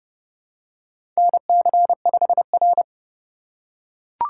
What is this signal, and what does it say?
Morse code, heard as a keyed tone, about 30 wpm, 700 Hz. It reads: NC5R
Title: USA Call Signs - 4 Characters - 2x Character Spacing - 30wpm